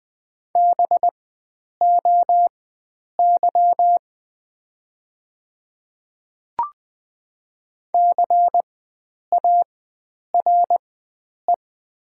Morse code, heard as a keyed tone, about 20 wpm, spaced as for 9 wpm, 700 Hz. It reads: BOY CARE
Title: 4x Character Spacing - Top 300 Words - 20wpm